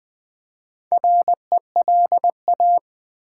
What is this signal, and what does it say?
Morse code, heard as a keyed tone, about 20 wpm, 700 Hz. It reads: RELA
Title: Build A Word - 1 Letter At A Time - 5-Letter Words - 20wpm